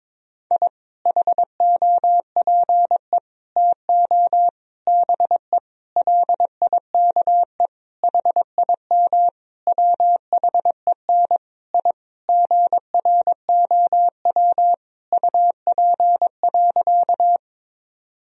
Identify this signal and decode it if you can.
Morse code, heard as a keyed tone, about 22 wpm, 700 Hz. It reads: I HOPE TO BE LIKE HIM WHEN I GROW UP.
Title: Sentences from Top 1000 Words - Review Part 1 - 22wpm